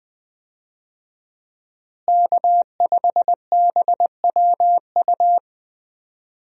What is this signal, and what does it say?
Morse code, heard as a keyed tone, about 21 wpm, 700 Hz. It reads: K5BWU
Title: USA Call Signs - 5 Characters - Repeated 2x - 20wpm